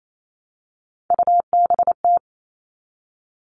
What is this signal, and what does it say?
Morse code, heard as a keyed tone, about 28 wpm, 700 Hz. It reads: UBT
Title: All Cut Numbers - 3-Digit 28wpm